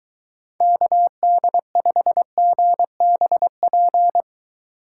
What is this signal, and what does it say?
Morse code, heard as a keyed tone, about 23 wpm, 700 Hz. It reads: KD5GBP